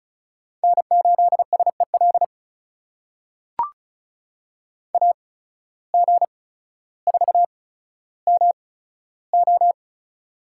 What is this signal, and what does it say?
Morse code, heard as a keyed tone, about 36 wpm, spaced as for 15 wpm, 700 Hz. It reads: N8SEL AG4MO
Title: USA Call Signs - 8x Character Spacing - 35wpm